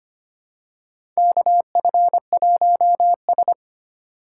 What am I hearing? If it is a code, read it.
Morse code, heard as a keyed tone, about 25 wpm, 700 Hz. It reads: KF1S
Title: USA Call Signs - 4 Characters - 4x Character Spacing - 25wpm